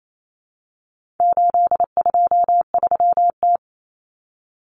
Morse code, heard as a keyed tone, about 28 wpm, 700 Hz. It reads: 823T
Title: Common Cut Numbers - 4-Digit 28wpm